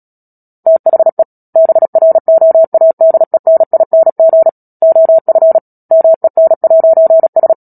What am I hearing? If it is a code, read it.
Morse code, heard as a keyed tone, about 37 wpm, 650 Hz. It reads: THE BROADENING OF MEN'S
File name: Book Two / 10-Chapter 10 - Epilogue - 36 WPM